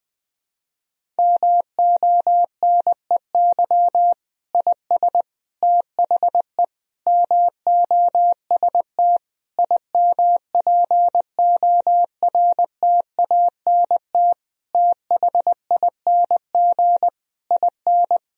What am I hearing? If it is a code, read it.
Morse code, heard as a keyed tone, about 20 wpm, 700 Hz. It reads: MONEY IS THE MOST IMPORTANT THING IN
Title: Sentences from Top 1000 Words - Review Part 1 - 20wpm